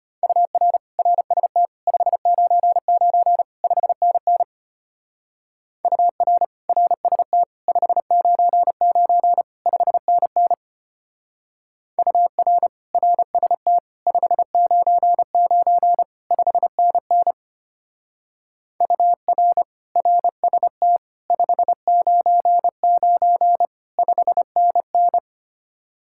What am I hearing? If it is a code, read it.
Morse code, heard as a keyed tone, about 32 wpm, 700 Hz. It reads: UR RST 599 5NN UR RST 599 5NN UR RST 599 5NN UR RST 599 5NN